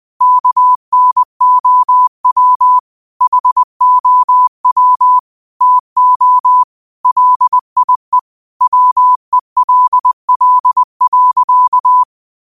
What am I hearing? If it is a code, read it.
Morse code, heard as a keyed tone, about 20 wpm, 1000 Hz. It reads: KNOW HOW TO LIE WELL.
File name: qotdfiles / 20wpm